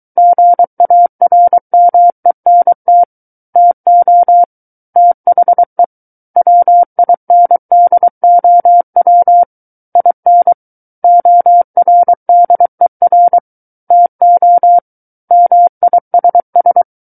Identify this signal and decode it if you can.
Morse code, heard as a keyed tone, about 23 wpm, 700 Hz. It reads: GARMENT TO THE WINDOW IN ORDER TO MISS
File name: Book One / 14-Chapter 14 - 23 WPM